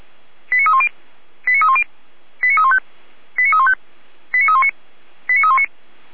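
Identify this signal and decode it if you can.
Five-tone selective call (ZVEI), heard as quick runs of five tones, tones of about 70 ms. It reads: 89310 89310 89315 89315 89319 89319
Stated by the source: ZVEI five-tone call received by SDR, from audio.f85360.feuerwehr-orlamuende.org, a fire brigade (Feuerwehr Orlamünde) server.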